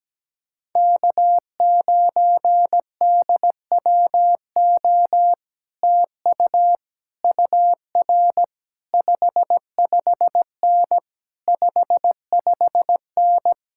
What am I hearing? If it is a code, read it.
Morse code, heard as a keyed tone, about 17 wpm, 700 Hz. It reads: K9DWO TU UR 55N 55N
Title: Speed-Racing Common QSO Phrases - 17wpm